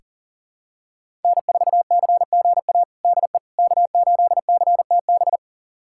Morse code, heard as a keyed tone, about 40 wpm, 700 Hz. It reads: N4CGA DE K8CTB
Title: USA Callsign DE Callsign - 5 Characters - 40wpm